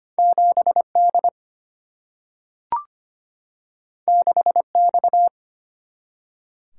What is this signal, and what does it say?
Morse code, heard as a keyed tone, about 25 wpm, 700 Hz. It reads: 7D 6X
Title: Single-Letter Single-Number 25wpm